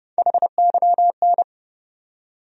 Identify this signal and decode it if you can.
Morse code, heard as a keyed tone, about 30 wpm, 700 Hz. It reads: HYN
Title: USA Call Sign Postfixes - 3 Letters 30wpm